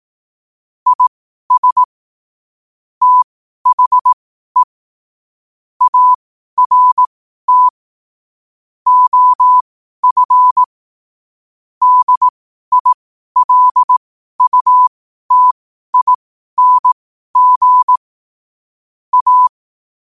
Morse code, heard as a keyed tone, about 18 wpm, spaced as for 12 wpm, 1000 Hz. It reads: IS THE ART OF DILUTING A
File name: qotdfiles / 12wpm